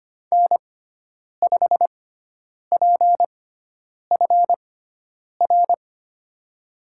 Morse code, heard as a keyed tone, about 25 wpm, spaced as for 9 wpm, 700 Hz. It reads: N5PFR
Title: USA Call Signs - 5 Characters - 6x Character Spacing - 25wpm